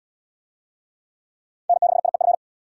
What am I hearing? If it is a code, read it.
Morse code, heard as a keyed tone, about 67 wpm, 700 Hz. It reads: N6IEF